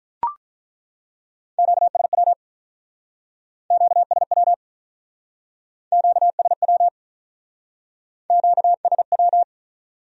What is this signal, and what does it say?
Morse code, heard as a keyed tone, about 42 wpm, 700 Hz. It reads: QSW QSW QSW QSW
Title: Speed-Racing Q-Codes - 35wpm